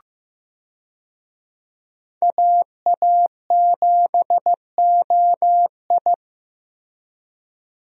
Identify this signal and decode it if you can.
Morse code, heard as a keyed tone, about 15 wpm, 700 Hz. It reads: AA7OI
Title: USA Call Signs - 5 Characters - Repeated 3x - 15wpm